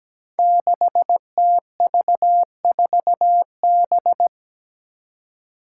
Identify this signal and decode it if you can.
Morse code, heard as a keyed tone, about 17 wpm, 700 Hz. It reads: 6TV4B
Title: Five Challenging Letter-Number Combinations 17wpm